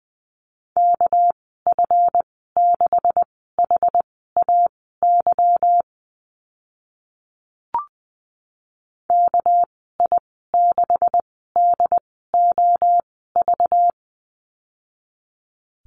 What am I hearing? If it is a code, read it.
Morse code, heard as a keyed tone, about 20 wpm, spaced as for 14 wpm, 700 Hz. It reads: KF6HAY KI6DOV
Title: 213 Lesson 34g - Number 8 - Call Signs 20wpm